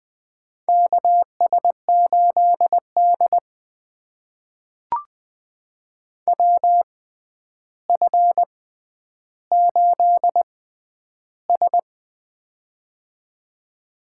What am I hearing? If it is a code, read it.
Morse code, heard as a keyed tone, about 20 wpm, spaced as for 10 wpm, 700 Hz. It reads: KS8D WF8S
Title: USA Call Signs - 4 Characters - 6x Character Spacing - 20wpm